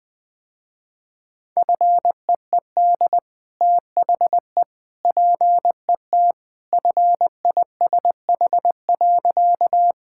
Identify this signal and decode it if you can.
Morse code, heard as a keyed tone, about 20 wpm, 700 Hz. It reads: FEED THE PET FISH.
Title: Easy Sentences with Hints - 20wpm